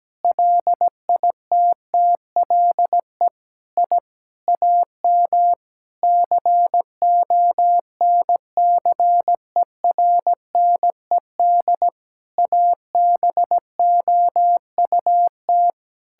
Morse code, heard as a keyed tone, about 17 wpm, 700 Hz. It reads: LITTLE I AM CONCERNED ABOUT
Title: Sentences from Top 1000 Words - Review Part 6 - 17wpm